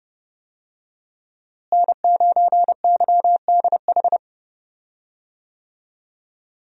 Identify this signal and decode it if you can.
Morse code, heard as a keyed tone, about 32 wpm, 700 Hz. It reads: N9YDH
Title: USA Call Signs - 5 Characters - Repeated 3x - 30wpm